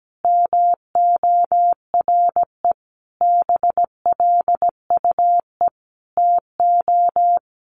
Morse code, heard as a keyed tone, about 17 wpm, 700 Hz. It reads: MORE BLUE TO
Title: Sentences from Top 1200 Words 17wpm